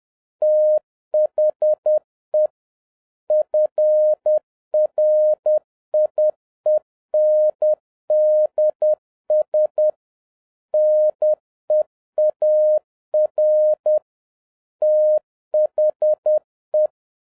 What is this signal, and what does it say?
Morse code, heard as a keyed tone, about 10 wpm, 610 Hz. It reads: THE FRIENDS NEAR THE
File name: 2011-07-mp3 / n36 THE 3.11 2 10wpm